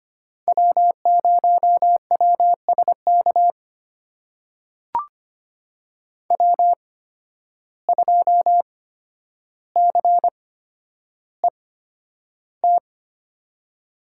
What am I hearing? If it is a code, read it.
Morse code, heard as a keyed tone, about 25 wpm, spaced as for 11 wpm, 700 Hz. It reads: W0WSK W2CET
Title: USA Call Signs - 5 Characters - 8x Character Spacing - 25wpm